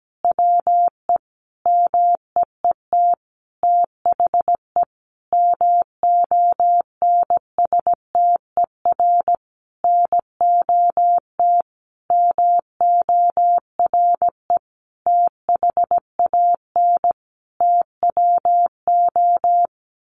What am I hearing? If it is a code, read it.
Morse code, heard as a keyed tone, about 17 wpm, 700 Hz. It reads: WE MEET THE MONSTER NOT MORE THAN TWO
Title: A Sound Of Thunder 17wpm